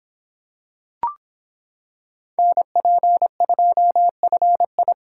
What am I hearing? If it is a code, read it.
Morse code, heard as a keyed tone, about 26 wpm, 700 Hz. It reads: NP2FI